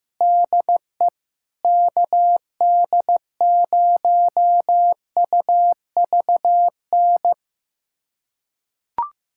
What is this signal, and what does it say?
Morse code, heard as a keyed tone, about 16 wpm, 700 Hz. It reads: DE KD0UVN
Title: USA Callsign DE Callsign 15wpm